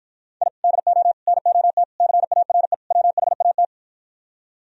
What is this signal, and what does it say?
Morse code, heard as a keyed tone, about 55 wpm, 700 Hz. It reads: I DO NOT CARE WHAT